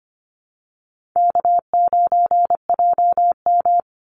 Morse code, heard as a keyed tone, about 25 wpm, 700 Hz. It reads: K9JM
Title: Easy Call Signs Repeated 2x - 25wpm